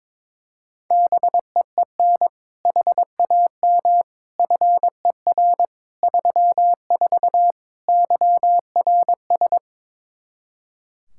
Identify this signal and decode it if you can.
Morse code, heard as a keyed tone, about 22 wpm, 700 Hz. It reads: BEEN HAM FER 34 YRS
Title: Practice QSO - Exchange 3 - More Info - 22wpm